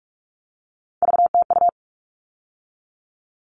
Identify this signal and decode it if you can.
Morse code, heard as a keyed tone, about 44 wpm, 700 Hz. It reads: VTU
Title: All Cut Numbers - 3-Digit 45wpm